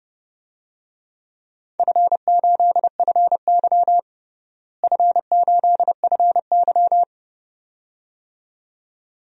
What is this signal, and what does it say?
Morse code, heard as a keyed tone, about 32 wpm, 700 Hz. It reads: F8FY F8FY
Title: DXCC-100 Call Signs - 4 Characters - Repeated 2x - 30wpm